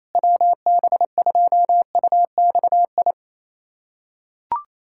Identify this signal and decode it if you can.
Morse code, heard as a keyed tone, about 28 wpm, 700 Hz. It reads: WB2UXI
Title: USA Callsign DE Callsign 28wpm